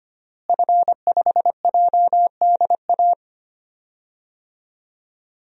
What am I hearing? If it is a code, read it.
Morse code, heard as a keyed tone, about 25 wpm, 700 Hz. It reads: F5JDA
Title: DXCC-100 Call Signs - 5 Characters - Repeated 3x - 25wpm